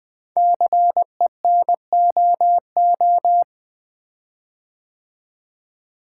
Morse code, heard as a keyed tone, about 20 wpm, 700 Hz. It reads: CENOO
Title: Code Groups - 5 Letters - 20wpm